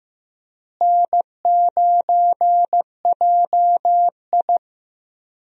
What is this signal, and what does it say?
Morse code, heard as a keyed tone, about 15 wpm, 700 Hz. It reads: N9JI
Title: USA Call Signs - 4 Characters - Repeated 3x - 15wpm